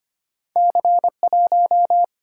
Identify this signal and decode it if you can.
Morse code, heard as a keyed tone, about 34 wpm, spaced as for 30 wpm, 690 Hz. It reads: C1
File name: practice / 25wpm / Speed-Racing - Two Letter-Number Combinations - 25wpm